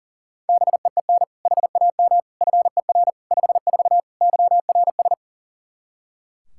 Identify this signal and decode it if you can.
Morse code, heard as a keyed tone, about 40 wpm, 700 Hz. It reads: BEEN HAM FER 54 YRS
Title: Practice QSO - Exchange 3 - More Info - 40wpm